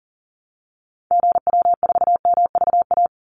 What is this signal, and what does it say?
Morse code, heard as a keyed tone, about 40 wpm, 700 Hz. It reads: GW4MVA
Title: Call Signs - 40wpm